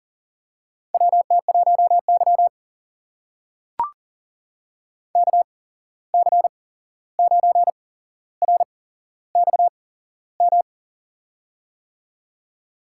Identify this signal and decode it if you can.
Morse code, heard as a keyed tone, about 38 wpm, spaced as for 15 wpm, 700 Hz. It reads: WT1Y KC9RXM
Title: USA Call Signs - 8x Character Spacing - 40wpm